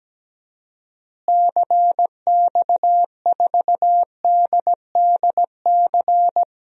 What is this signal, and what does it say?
Morse code, heard as a keyed tone, about 17 wpm, 700 Hz. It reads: CX4DDC
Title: DXCC-100 Call Signs - 6 Characters - Repeated 3x - 17wpm